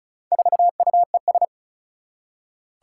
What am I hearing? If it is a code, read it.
Morse code, heard as a keyed tone, about 35 wpm, 690 Hz. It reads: 4UES
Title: Four Challenging Letter-Number Combinations 35wpm